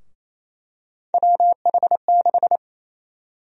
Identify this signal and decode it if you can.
Morse code, heard as a keyed tone, about 28 wpm, 700 Hz. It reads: WH6
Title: USA Call Sign Prefixes - Repeated 2x - 28wpm